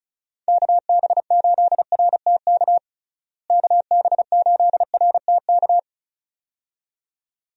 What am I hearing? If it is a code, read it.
Morse code, heard as a keyed tone, about 36 wpm, 700 Hz. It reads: KB8RTK KB8RTK
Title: USA Call Signs - 6 Characters - Repeated 2x - 35wpm